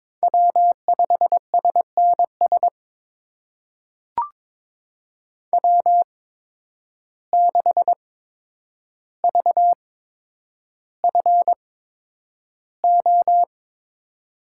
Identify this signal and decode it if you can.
Morse code, heard as a keyed tone, about 21 wpm, spaced as for 9 wpm, 700 Hz. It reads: W5SNS W6VFO
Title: USA Call Signs - 5 Characters - 8x Character Spacing - 22wpm